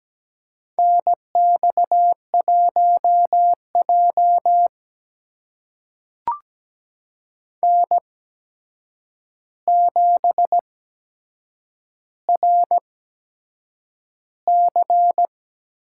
Morse code, heard as a keyed tone, about 17 wpm, spaced as for 7 wpm, 700 Hz. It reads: NX1J N7RC
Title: USA Call Signs - 4 Characters - 8x Character Spacing - 17wpm